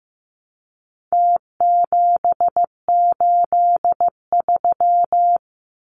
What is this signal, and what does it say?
Morse code, heard as a keyed tone, about 15 wpm, 700 Hz. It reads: T783
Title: Numbers - Contest Style - 4-Digit 15wpm